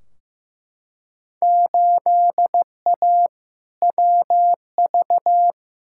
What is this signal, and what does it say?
Morse code, heard as a keyed tone, about 15 wpm, 700 Hz. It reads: 8A WV
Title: Field Day Exchanges - 15wpm